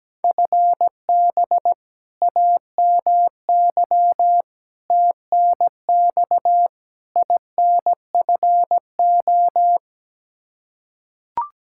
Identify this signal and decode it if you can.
Morse code, heard as a keyed tone, about 17 wpm, 700 Hz. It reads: FB AMY TNX INFO
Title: Practice QSO - Exchange 3 - More Info - 17wpm